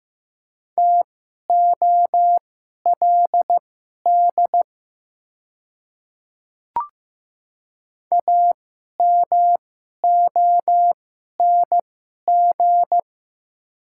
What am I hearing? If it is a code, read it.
Morse code, heard as a keyed tone, about 15 wpm, spaced as for 11 wpm, 690 Hz. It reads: TOLD AMONG
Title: 2x Character Spacing - Top 500 Words - 15wpm